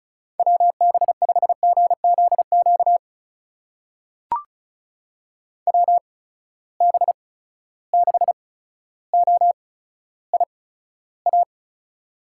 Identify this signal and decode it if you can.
Morse code, heard as a keyed tone, about 36 wpm, spaced as for 15 wpm, 700 Hz. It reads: WB5GZQ WB6OIA
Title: USA Call Signs - 6 Characters - 8x Character Spacing - 35wpm